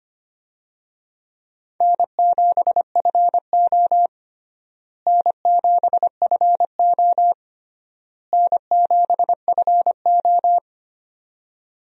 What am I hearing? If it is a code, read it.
Morse code, heard as a keyed tone, about 25 wpm, 700 Hz. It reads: N7FO N7FO N7FO
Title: USA Call Signs - 4 Characters - Repeated 3x - 25wpm